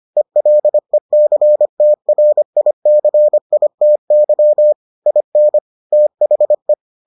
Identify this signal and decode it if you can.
Morse code, heard as a keyed tone, about 25 wpm, 600 Hz. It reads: ELECTRICITY IN THE